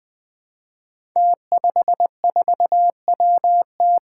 Morse code, heard as a keyed tone, about 20 wpm, 700 Hz. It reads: T54WT
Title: Five Challenging Letter-Number Combinations 20wpm